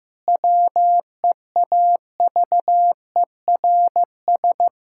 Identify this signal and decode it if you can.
Morse code, heard as a keyed tone, about 15 wpm, 700 Hz. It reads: WEAVERS
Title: Pangram Challenge v2 - 15wpm